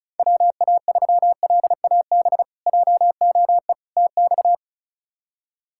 Morse code, heard as a keyed tone, about 36 wpm, 700 Hz. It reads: WA3LAB JOE TX
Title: K1USN SST QSO Practice - 35wpm